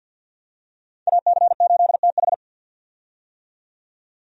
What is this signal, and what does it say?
Morse code, heard as a keyed tone, about 52 wpm, 700 Hz. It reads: AC8TH
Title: USA Call Signs - 5 Characters - 50wpm